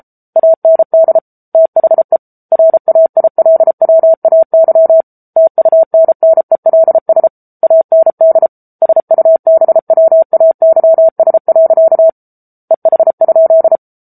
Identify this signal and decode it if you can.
Morse code, heard as a keyed tone, about 33 wpm, 650 Hz. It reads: AND THE RAILWAY TUNNELS AND SUBWAYS. EH?